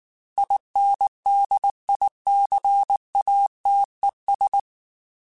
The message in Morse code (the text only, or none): INDICATES